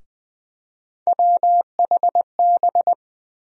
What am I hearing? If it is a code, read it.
Morse code, heard as a keyed tone, about 21 wpm, 700 Hz. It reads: WHB
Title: USA Call Sign Postfixes - 3 Letters 20wpm